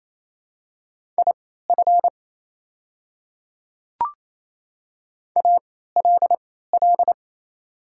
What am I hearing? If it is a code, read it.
Morse code, heard as a keyed tone, about 27 wpm, spaced as for 16 wpm, 700 Hz. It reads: IF ALL
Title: 3x Character Spacing - Top 100 Words - 28wpm